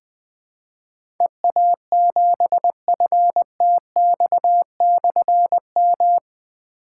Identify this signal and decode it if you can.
Morse code, heard as a keyed tone, about 20 wpm, 700 Hz. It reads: EA7FTX/M
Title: Difficult Call Signs 20wpm